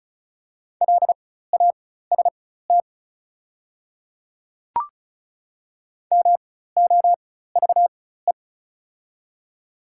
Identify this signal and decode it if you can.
Morse code, heard as a keyed tone, about 35 wpm, spaced as for 16 wpm, 700 Hz. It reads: LAST MOVE
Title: 4x Character Spacing - Top 300 Words - 35wpm